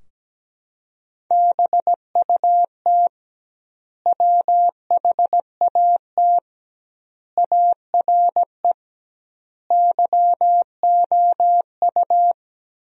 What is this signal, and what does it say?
Morse code, heard as a keyed tone, about 17 wpm, 700 Hz. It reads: BUT WHAT ARE YOU
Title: 2x Word-Spacing - Sentences from Top 100 Words - 17wpm